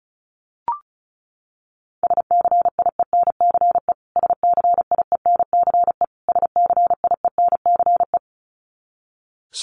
Morse code, heard as a keyed tone, about 34 wpm, 700 Hz. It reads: SCIENCE SCIENCE SCIENCE
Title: Top 500 Words Repeated 35wpm